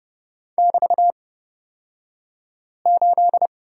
Morse code, heard as a keyed tone, about 30 wpm, spaced as for 5 wpm, 690 Hz. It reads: <BT>8
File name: Lesson 36d - BT - Two Characters 30wpm